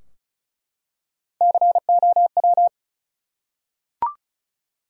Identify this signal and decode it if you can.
Morse code, heard as a keyed tone, about 36 wpm, 700 Hz. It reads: COW
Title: Three Letter Words - 35wpm